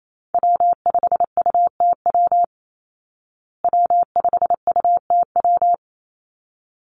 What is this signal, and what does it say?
Morse code, heard as a keyed tone, about 28 wpm, 700 Hz. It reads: W5UTW W5UTW
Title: Call Signs Repeated 3x - 28wpm